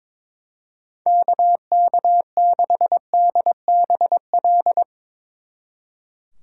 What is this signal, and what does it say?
Morse code, heard as a keyed tone, about 22 wpm, 700 Hz. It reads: KK6DBL
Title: Speed Racing - USA 2x3 Call Signs 22wpm